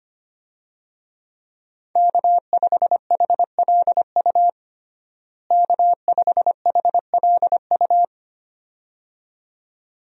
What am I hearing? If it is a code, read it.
Morse code, heard as a keyed tone, about 25 wpm, 700 Hz. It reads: K5HLU K5HLU
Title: USA Call Signs - 5 Characters - Repeated 2x - 25wpm